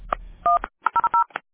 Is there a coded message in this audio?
Yes: DTMF from 0.4 to 1.2 s, heard as telephone touch tones, tones of about 130 ms. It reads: 100